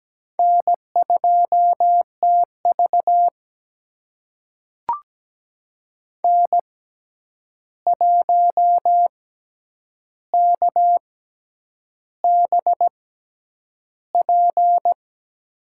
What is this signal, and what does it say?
Morse code, heard as a keyed tone, about 17 wpm, spaced as for 8 wpm, 700 Hz. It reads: N2TV N1KBP
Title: USA Call Signs - 6x Character Spacing - 17wpm